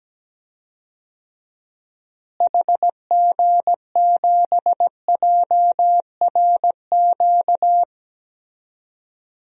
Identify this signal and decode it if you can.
Morse code, heard as a keyed tone, about 17 wpm, 700 Hz. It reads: HG7JRQ
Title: DXCC-100 Call Signs - 6 Characters - Repeated 2x - 17wpm